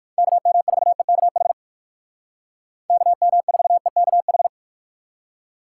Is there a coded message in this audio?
Yes: Morse code from 0.2 to 4.5 s, heard as a keyed tone, about 58 wpm, 700 Hz. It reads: KM4EKH KM4EKH